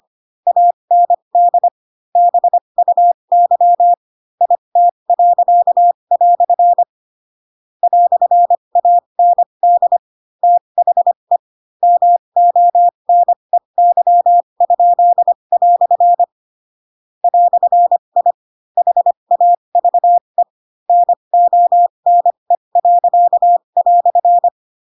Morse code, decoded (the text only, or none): AND BUY IT." "AND THE MONEY?" "I HAVE NONE."